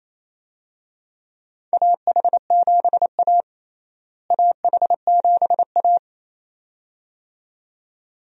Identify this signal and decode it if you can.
Morse code, heard as a keyed tone, about 28 wpm, 700 Hz. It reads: AH7A AH7A
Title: USA Call Signs - 4 Characters - Repeated 2x - 28wpm